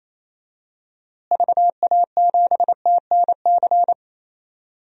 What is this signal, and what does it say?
Morse code, heard as a keyed tone, about 28 wpm, 700 Hz. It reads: VA7TNC
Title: DXCC-100 Call Signs - 6 Characters - Repeated 3x - 28wpm